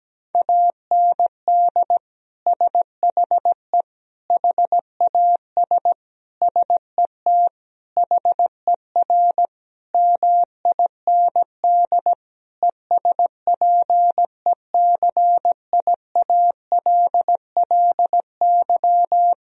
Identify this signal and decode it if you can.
Morse code, heard as a keyed tone, about 17 wpm, 700 Hz. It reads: AND SHE HAS SET HER MIND ESPECIALLY
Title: Sentences from Top 1000 Words - Review Part 4 - 17wpm